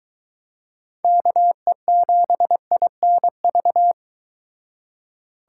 Morse code, heard as a keyed tone, about 23 wpm, 700 Hz. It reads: KE7INV